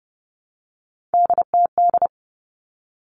Morse code, heard as a keyed tone, about 30 wpm, 700 Hz. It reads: DTD
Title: All Cut Numbers - 3-Digit 30wpm